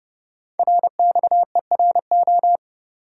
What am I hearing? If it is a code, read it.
Morse code, heard as a keyed tone, about 30 wpm, 700 Hz. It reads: RXERO
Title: Code Groups - 5 Letters - 30wpm